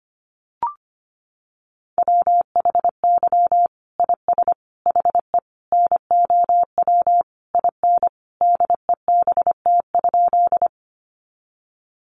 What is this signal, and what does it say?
Morse code, heard as a keyed tone, about 25 wpm, 700 Hz. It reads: WHY IS HE NOW IN DEBT?
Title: Sentences from Top 1300 Words 25wpm